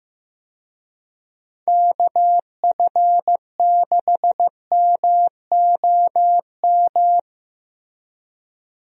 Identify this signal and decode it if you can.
Morse code, heard as a keyed tone, about 16 wpm, 700 Hz. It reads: KF6MOM
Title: USA Call Signs - 6 Characters - Repeated 3x - 15wpm